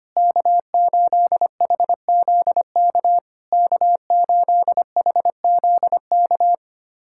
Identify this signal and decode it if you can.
Morse code, heard as a keyed tone, about 25 wpm, 700 Hz. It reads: K8HZK K8HZK
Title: SOTA QSO Practice 25wpm